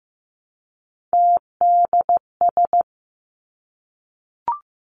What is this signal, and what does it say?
Morse code, heard as a keyed tone, about 15 wpm, 700 Hz. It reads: TDS
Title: Three Challenging Letter-Number Combinations 15wpm